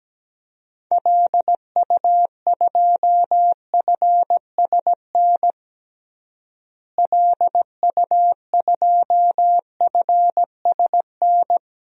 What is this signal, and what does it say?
Morse code, heard as a keyed tone, about 17 wpm, 700 Hz. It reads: LU2FSN LU2FSN
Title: DXCC-100 Call Signs - 6 Characters - Repeated 3x - 17wpm